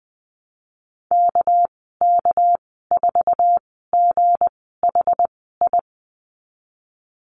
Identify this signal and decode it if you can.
Morse code, heard as a keyed tone, about 20 wpm, spaced as for 14 wpm, 700 Hz. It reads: KK4GHI
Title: 207 Lesson 33g - Letter j - Call Signs 20wpm